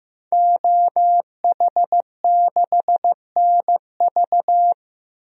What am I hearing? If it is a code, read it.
Morse code, heard as a keyed tone, about 15 wpm, 700 Hz. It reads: OH6NV
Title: DXCC-100 Call Signs - 5 Characters - 15wpm